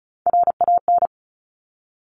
Morse code, heard as a keyed tone, about 36 wpm, 700 Hz. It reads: RAN
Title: Top Three Letter Combinations in Words 35wpm